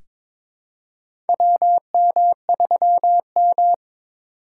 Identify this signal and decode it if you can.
Morse code, heard as a keyed tone, about 22 wpm, 700 Hz. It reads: WM3M
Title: USA Call Signs - 4 Characters - 6x Character Spacing - 22wpm